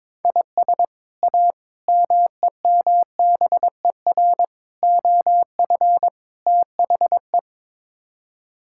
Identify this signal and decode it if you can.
Morse code, heard as a keyed tone, about 22 wpm, 700 Hz. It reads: IS A MEMBER OF THE
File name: practice / 22wpm / Build A Phrase - 5-Word Phrases - 22wpm